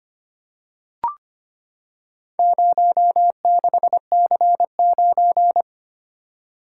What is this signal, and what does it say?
Morse code, heard as a keyed tone, about 25 wpm, 700 Hz. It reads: 06C9
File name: practice / 25wpm / Four Letter-Number Combinations - 25wpm